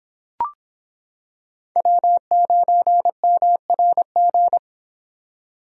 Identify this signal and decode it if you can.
Morse code, heard as a keyed tone, about 26 wpm, 700 Hz. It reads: W9MRG